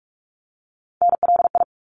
Morse code, heard as a keyed tone, about 44 wpm, 700 Hz. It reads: NLI
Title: ARRL Sections 45wpm